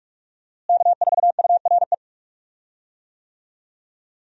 Morse code, heard as a keyed tone, about 42 wpm, 700 Hz. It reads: K4URE
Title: USA Call Signs - 5 Characters - Repeated 3x - 45wpm